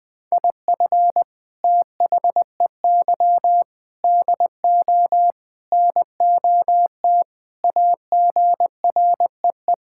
Morse code, heard as a keyed tone, about 20 wpm, 700 Hz. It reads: IF THEY DO NOT AGREE
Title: Sentences with 4-Word Phrases - Spoken Beginning - 20wpm